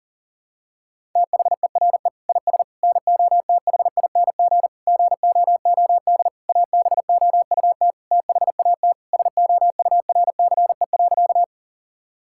Morse code, heard as a keyed tone, about 40 wpm, 700 Hz. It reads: THERE IS NOTHING GOOD ABOUT THAT SOURCE.
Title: Sentences from Top 1000 Words - Review Part 5 - 40wpm